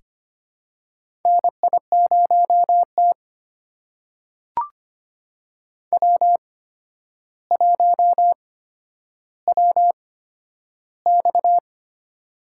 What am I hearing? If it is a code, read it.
Morse code, heard as a keyed tone, about 25 wpm, spaced as for 11 wpm, 700 Hz. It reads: NI0T W1WX
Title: USA Call Signs - 4 Characters - 8x Character Spacing - 25wpm